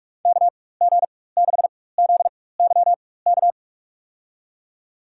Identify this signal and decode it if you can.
Morse code, heard as a keyed tone, about 46 wpm, spaced as for 21 wpm, 700 Hz. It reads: KG6ZYK
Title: USA Call Signs - 6 Characters - 4x Character Spacing - 45wpm